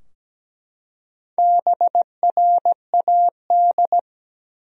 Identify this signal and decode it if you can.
Morse code, heard as a keyed tone, about 17 wpm, 700 Hz. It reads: BRAD
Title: USA Common Names - Max 4 Letters - 17wpm